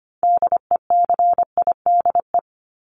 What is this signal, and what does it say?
Morse code, heard as a keyed tone, about 25 wpm, 700 Hz. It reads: DECIDE
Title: ED Suffix - Top 100 Words with Base Words 25wpm